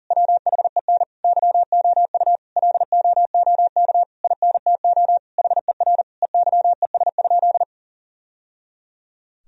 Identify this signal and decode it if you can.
Morse code, heard as a keyed tone, about 40 wpm, 700 Hz. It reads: WHEN YOU LOOK INTO HER EYES?
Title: Sentences from Top 1000 Words - Review Part 4 - 40wpm